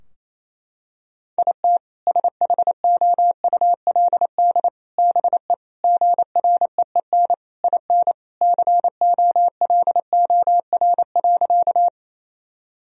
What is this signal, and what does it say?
Morse code, heard as a keyed tone, about 28 wpm, 700 Hz. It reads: IT SHOULD BE GREEN IN COLOR.
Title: Sentences from Top 1000 Words - Review Part 2 - 28wpm